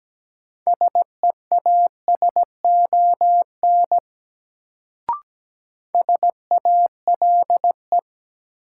Morse code, heard as a keyed tone, about 17 wpm, 700 Hz. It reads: SEASON SALE
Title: Sentences from Top 1400 Words - New Word Drill Alt - 17wpm